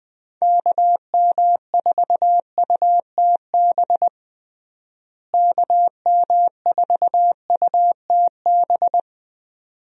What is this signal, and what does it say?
Morse code, heard as a keyed tone, about 21 wpm, 700 Hz. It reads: KM4UTB KM4UTB
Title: USA Call Signs - 6 Characters - Repeated 2x - 20wpm